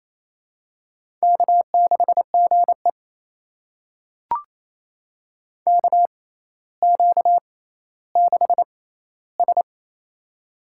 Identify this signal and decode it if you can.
Morse code, heard as a keyed tone, about 27 wpm, spaced as for 14 wpm, 700 Hz. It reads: K6GE KQ6S
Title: USA Call Signs - 6x Character Spacing - 28wpm